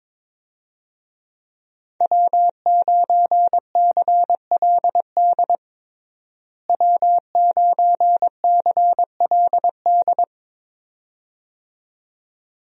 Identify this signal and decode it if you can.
Morse code, heard as a keyed tone, about 21 wpm, 700 Hz. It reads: W9CLD W9CLD
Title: USA Call Signs - 5 Characters - Repeated 2x - 22wpm